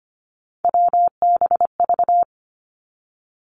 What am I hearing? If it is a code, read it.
Morse code, heard as a keyed tone, about 25 wpm, 700 Hz. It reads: WBV
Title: Three Challenging Letter-Number Combinations 25wpm